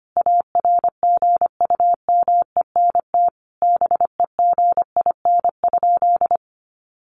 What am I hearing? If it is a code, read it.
Morse code, heard as a keyed tone, about 25 wpm, 700 Hz. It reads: ARGUMENT BEGIN?
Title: Sentences from Top 1400 Words 25wpm